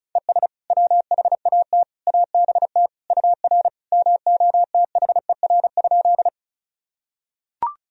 Morse code, heard as a keyed tone, about 34 wpm, 700 Hz. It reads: ES WHAT ABT UR MOTHER?
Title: Sentences from Top 200 Words and Abbreviations 35wpm